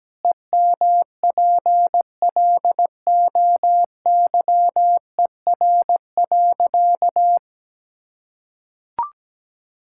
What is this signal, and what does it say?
Morse code, heard as a keyed tone, about 17 wpm, 700 Hz. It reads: EMPLOYER.
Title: Pangram Challenge v2 - 17wpm